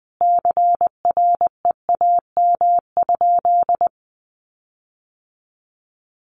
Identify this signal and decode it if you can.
Morse code, heard as a keyed tone, about 20 wpm, 700 Hz. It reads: CREAM?
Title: Sentences from Top 1500 Words 20wpm